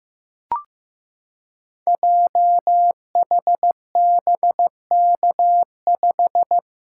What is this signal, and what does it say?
Morse code, heard as a keyed tone, about 15 wpm, 700 Hz. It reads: JHBK5
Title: Code Groups - 5 Letter:Numbers - 15wpm